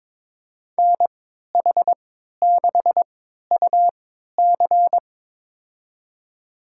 Morse code, heard as a keyed tone, about 23 wpm, spaced as for 13 wpm, 700 Hz. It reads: NH6UC
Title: USA Call Signs - 5 Characters - 3x Character Spacing - 22wpm